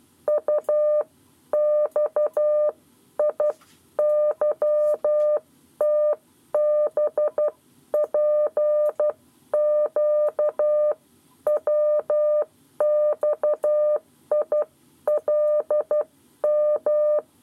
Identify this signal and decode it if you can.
Morse code, heard as a keyed tone, about 12 wpm, spaced as for 10 wpm, 600 Hz. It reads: UXIYTBPQWXILM
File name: 7 Lezione cw 1